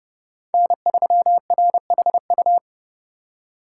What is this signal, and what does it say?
Morse code, heard as a keyed tone, about 31 wpm, 700 Hz. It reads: N3RHU